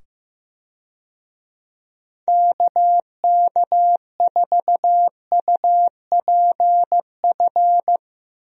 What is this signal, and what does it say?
Morse code, heard as a keyed tone, about 15 wpm, 700 Hz. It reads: KK4UPF
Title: USA Call Signs - 6 Characters - 15wpm